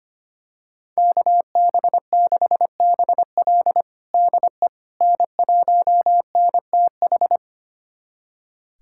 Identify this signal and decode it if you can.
Morse code, heard as a keyed tone, about 25 wpm, 700 Hz. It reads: KB6BL DE N1NTH
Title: USA Callsign DE Callsign - 5 Characters - 25wpm